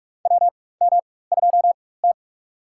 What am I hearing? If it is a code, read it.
Morse code, heard as a keyed tone, about 45 wpm, spaced as for 21 wpm, 690 Hz. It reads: WM2T
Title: USA Call Signs - 4x Character Spacing - 45wpm